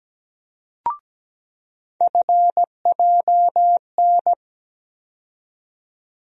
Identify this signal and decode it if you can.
Morse code, heard as a keyed tone, about 17 wpm, 700 Hz. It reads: FJN
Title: USA Call Sign Postfixes - 3 Letters 17wpm